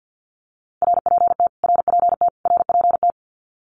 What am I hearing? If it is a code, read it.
Morse code, heard as a keyed tone, about 50 wpm, 700 Hz. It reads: RPT RPT RPT
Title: Abbreviations Repeated 3x - 50wpm